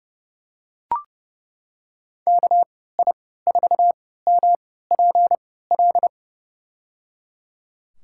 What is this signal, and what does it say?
Morse code, heard as a keyed tone, about 30 wpm, spaced as for 17 wpm, 700 Hz. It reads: KI4MPL
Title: USA Call Signs - 6 Characters - 3x Character Spacing - 30wpm